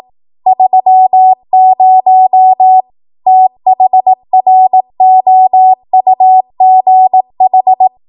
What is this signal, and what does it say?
Morse code, heard as a keyed tone, about 18 wpm, 750 Hz. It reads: 30 THROUGH
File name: Archive / 18 WPM / 18WPM